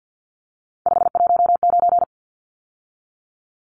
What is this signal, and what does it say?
Morse code, heard as a keyed tone, about 50 wpm, 700 Hz. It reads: 519
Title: Numbers - 3-Digit 50wpm